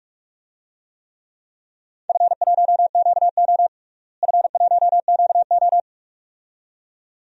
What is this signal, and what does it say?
Morse code, heard as a keyed tone, about 42 wpm, 700 Hz. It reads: F1QO F1QO
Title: DXCC-100 Call Signs - 4 Characters - Repeated 2x - 45wpm